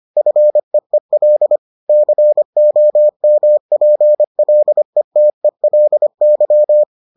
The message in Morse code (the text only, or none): FEEL COMPLETELY